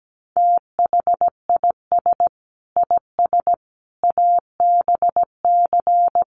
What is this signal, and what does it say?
Morse code, heard as a keyed tone, about 17 wpm, 700 Hz. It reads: THIS IS ABC
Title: Sets of 5 Words 17wpm